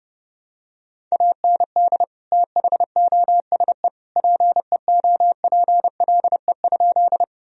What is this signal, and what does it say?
Morse code, heard as a keyed tone, about 30 wpm, 700 Hz. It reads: AND THOSE PEOPLE?
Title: Sentences from Top 1000 Words - Review Part 1 - 30wpm